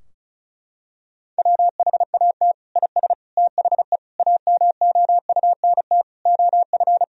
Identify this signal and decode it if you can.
Morse code, heard as a keyed tone, about 34 wpm, 700 Hz. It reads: WHAT IS THE AMOUNT OF
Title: Sentences from Top 1000 Words - Review Part 4 - 35wpm